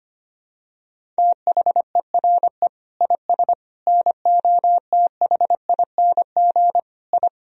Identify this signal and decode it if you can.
Morse code, heard as a keyed tone, about 25 wpm, 700 Hz. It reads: THERE IS NOTHING I
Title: Sentences from Top 1000 Words - Review Part 1 - 25wpm